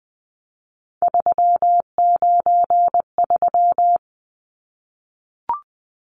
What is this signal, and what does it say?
Morse code, heard as a keyed tone, about 20 wpm, 700 Hz. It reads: 393
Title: Numbers - 3-Digit 20wpm